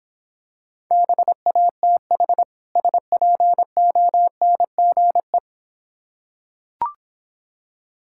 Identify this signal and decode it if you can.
Morse code, heard as a keyed tone, about 26 wpm, 700 Hz. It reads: BATH SPONGE